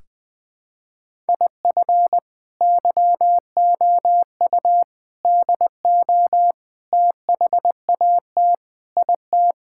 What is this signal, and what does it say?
Morse code, heard as a keyed tone, about 20 wpm, 700 Hz. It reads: IF YOU DO THAT IT
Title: Sentences with 3-Word Phrases - Spoken Beginning - 20wpm